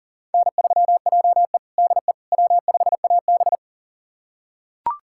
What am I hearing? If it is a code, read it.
Morse code, heard as a keyed tone, about 40 wpm, 700 Hz. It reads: N3JE DE W5AB
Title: USA Callsign DE Callsign - 4 Characters - 40wpm